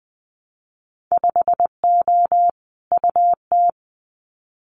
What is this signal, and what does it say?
Morse code, heard as a keyed tone, about 20 wpm, 700 Hz. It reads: 5O UT
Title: Winter Field Day Exchanges 20wpm